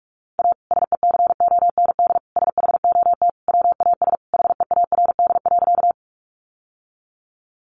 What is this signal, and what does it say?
Morse code, heard as a keyed tone, about 44 wpm, 700 Hz. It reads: A SECOND SHOT WAS HEARD.
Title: Sentences from Top 1100 Words 45wpm